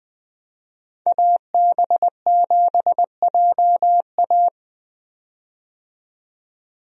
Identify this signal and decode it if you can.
Morse code, heard as a keyed tone, about 20 wpm, 700 Hz. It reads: AB7JA
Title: USA Call Signs - 5 Characters - 20wpm